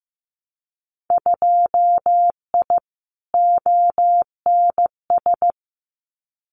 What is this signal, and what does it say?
Morse code, heard as a keyed tone, about 15 wpm, 700 Hz. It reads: 2I ONS
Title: Winter Field Day Exchanges Encore 15wpm